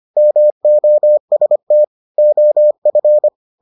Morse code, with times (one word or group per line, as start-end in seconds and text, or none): 0.2-1.8 MOST
2.2-3.3 OF